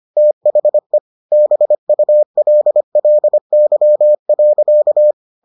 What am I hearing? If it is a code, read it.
Morse code, heard as a keyed tone, about 25 wpm, 600 Hz. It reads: THE BULLY.